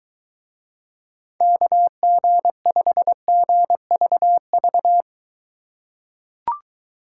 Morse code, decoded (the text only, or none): KG5GVV